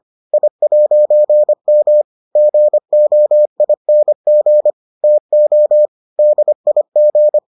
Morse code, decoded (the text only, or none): I'M GOING TO DIG